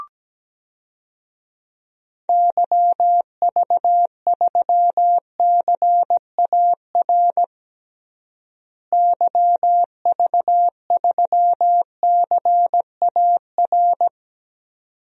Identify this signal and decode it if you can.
Morse code, heard as a keyed tone, about 17 wpm, 700 Hz. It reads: YV3CAR YV3CAR
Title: DXCC-100 Call Signs - 6 Characters - Repeated 2x - 17wpm